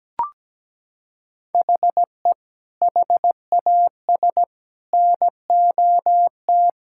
Morse code, heard as a keyed tone, about 17 wpm, 700 Hz. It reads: HE HAS NOT
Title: Sentences from Top 1000 Words - Review Part 6 - 17wpm